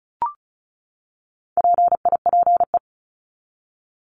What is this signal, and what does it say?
Morse code, heard as a keyed tone, about 34 wpm, 700 Hz. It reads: PIPE
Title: Top 1251-1500 Words 35wpm